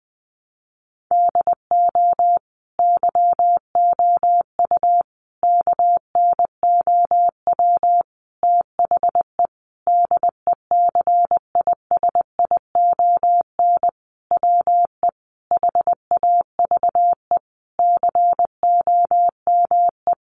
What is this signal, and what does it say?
Morse code, heard as a keyed tone, about 20 wpm, 700 Hz. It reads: DO YOU KNOW THE DECISION WE HAVE COME
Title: Sentences from Top 1000 Words 20wpm